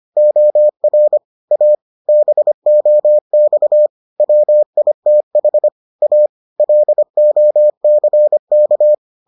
Morse code, heard as a keyed tone, about 25 wpm, 600 Hz. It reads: OR A BOX WITH A LOCK